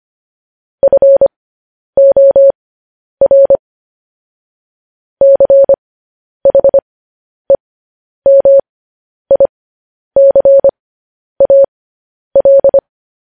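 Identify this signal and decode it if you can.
Morse code, heard as a keyed tone, about 25 wpm, spaced as for 10 wpm, 550 Hz. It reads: FOR CHEMICAL
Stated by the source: Morse code sending of news headlines, speed 10.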